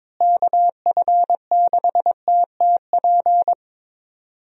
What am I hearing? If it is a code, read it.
Morse code, heard as a keyed tone, about 22 wpm, 700 Hz. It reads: KF6TTP
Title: USA Call Signs - 6 Characters - Repeated 3x - 22wpm